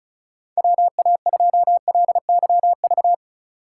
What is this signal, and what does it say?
Morse code, heard as a keyed tone, about 36 wpm, 700 Hz. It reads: WA2LYV
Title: USA Call Signs - 6 Characters - 3x Character Spacing - 35wpm